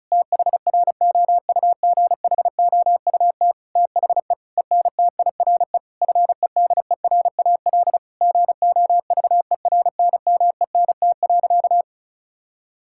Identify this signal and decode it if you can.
Morse code, heard as a keyed tone, about 36 wpm, 700 Hz. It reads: THROUGHOUT THE ENTIRE FEDERAL GOVERNMENT.
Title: Sentences from Top 1000 Words - Review Part 5 - 35wpm